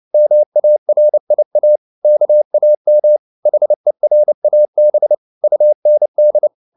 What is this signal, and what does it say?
Morse code, heard as a keyed tone, about 29 wpm, 600 Hz. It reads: MARIA KAM HERAB UND